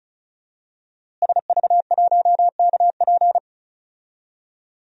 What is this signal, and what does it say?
Morse code, heard as a keyed tone, about 36 wpm, 700 Hz. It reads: SV1KP
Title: DXCC-100 Call Signs - 5 Characters - Repeated 2x - 35wpm